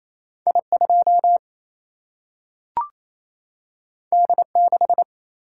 Morse code, heard as a keyed tone, about 28 wpm, 700 Hz. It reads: I2 D6
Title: Single-Letter Single-Number 28wpm